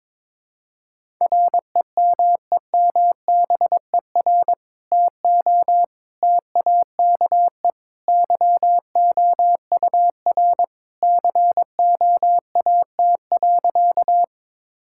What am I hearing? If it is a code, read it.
Morse code, heard as a keyed tone, about 21 wpm, 700 Hz. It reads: REMEMBER TO TAKE YOUR COAT.
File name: practice / 22wpm / Sentences with 2-Word Phrases - 22wpm